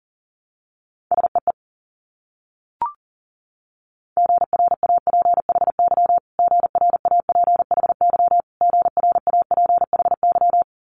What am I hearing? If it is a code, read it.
Morse code, heard as a keyed tone, about 43 wpm, 700 Hz. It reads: SEE GRAPHY GRAPHY GRAPHY
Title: Multi-letter Phonemes Repeated 3x 40wpm